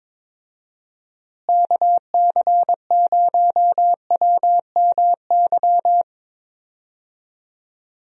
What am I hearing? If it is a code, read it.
Morse code, heard as a keyed tone, about 22 wpm, 700 Hz. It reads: KC0WMY
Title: USA Call Signs - 6 Characters - Repeated 2x - 22wpm